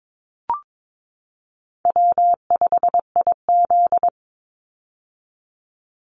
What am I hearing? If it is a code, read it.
Morse code, heard as a keyed tone, about 22 wpm, 700 Hz. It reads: W5IZ
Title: Call Signs - 22wpm